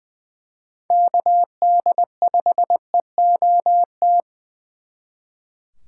Morse code, heard as a keyed tone, about 24 wpm, 700 Hz. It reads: KD5EOT
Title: Speed Racing - USA 2x3 Call Signs 20wpm